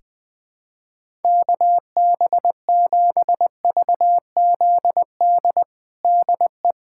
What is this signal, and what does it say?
Morse code, heard as a keyed tone, about 20 wpm, 700 Hz. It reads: KB7VZD DE
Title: USA Callsign DE Callsign 20wpm